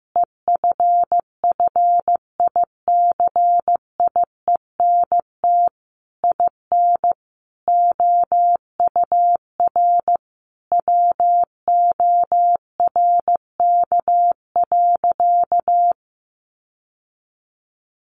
Morse code, heard as a keyed tone, about 15 wpm, 700 Hz. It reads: EFFICIENT IN OUR WORK.
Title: Sentences from Top 1300 Words 15wpm